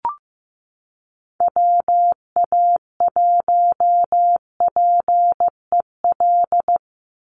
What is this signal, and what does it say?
Morse code, heard as a keyed tone, about 15 wpm, 700 Hz. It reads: WA1PEL
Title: USA Call Signs 15wpm